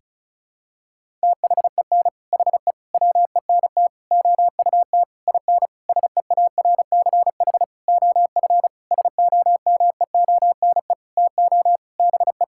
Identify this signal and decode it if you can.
Morse code, heard as a keyed tone, about 36 wpm, 700 Hz. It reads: THEN HE WENT OUT IN SEARCH OF SOMEONE TO BE
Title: Sentences from Top 1000 Words - Review Part 5 - 35wpm